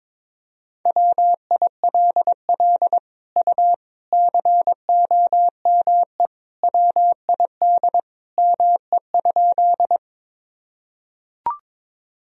Morse code, heard as a keyed tone, about 22 wpm, 700 Hz. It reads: WILL U COME WID ME?
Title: Sentences from Top 200 Words and Abbreviations 22wpm